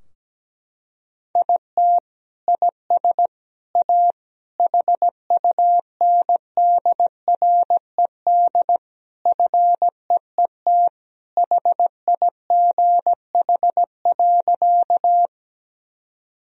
Morse code, heard as a keyed tone, about 17 wpm, 700 Hz. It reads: IT IS A HUNDRED FEET HIGH.
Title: Sentences from Top 1000 Words - Review Part 2 - 17wpm